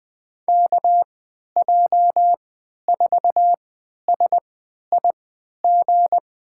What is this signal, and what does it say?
Morse code, heard as a keyed tone, about 21 wpm, spaced as for 11 wpm, 700 Hz. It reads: KJ4SIG
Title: USA Call Signs - 6 Characters - 3x Character Spacing - 20wpm